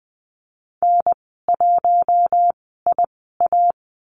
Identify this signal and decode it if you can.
Morse code, heard as a keyed tone, about 20 wpm, spaced as for 15 wpm, 700 Hz. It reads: N1IA
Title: 042 Lesson 8g - Number 1 - Call Signs 20wpm